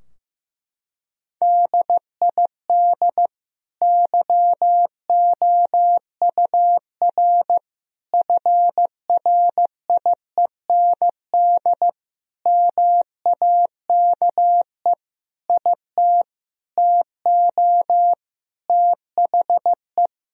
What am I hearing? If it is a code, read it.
Morse code, heard as a keyed tone, about 15 wpm, 700 Hz. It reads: DID YOUR FRIEND MAKE IT TO THE
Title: Sentences from Top 1000 Words - Review Part 3 - 15wpm